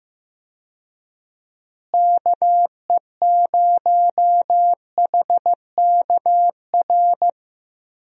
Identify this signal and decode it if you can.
Morse code, heard as a keyed tone, about 16 wpm, 700 Hz. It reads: KE0HKR
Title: USA Call Signs - 6 Characters - Repeated 3x - 15wpm